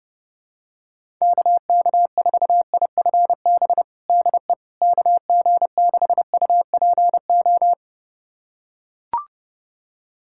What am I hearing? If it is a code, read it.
Morse code, heard as a keyed tone, about 30 wpm, 700 Hz. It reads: KK4IFB DE KG6UPO
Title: USA Callsign DE Callsign - 6 Characters - 30wpm